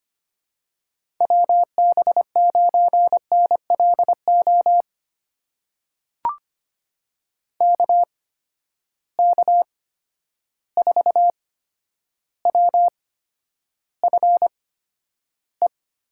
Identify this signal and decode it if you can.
Morse code, heard as a keyed tone, about 25 wpm, spaced as for 11 wpm, 700 Hz. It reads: WB9NLO KK4WFE
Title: USA Call Signs - 6 Characters - 8x Character Spacing - 25wpm